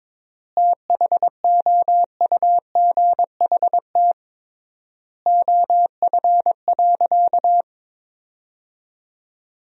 Morse code, heard as a keyed tone, about 22 wpm, 700 Hz. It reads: THOUGHT OF.
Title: 3x Word-Spacing - Sentences from Top 300 Words - 22wpm